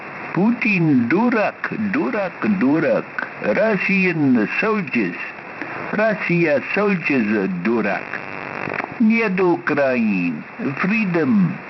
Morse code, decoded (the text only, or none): none